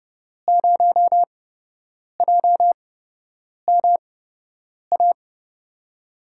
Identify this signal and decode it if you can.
Morse code, heard as a keyed tone, about 30 wpm, spaced as for 8 wpm, 700 Hz. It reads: 0JMA
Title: USA Call Signs - 6 Characters - 8x Character Spacing - 30wpm